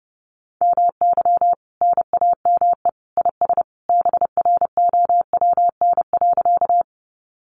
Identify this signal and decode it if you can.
Morse code, heard as a keyed tone, about 30 wpm, 700 Hz. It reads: MY NAME IS BROWN.
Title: Sentences from Top 1200 Words 30wpm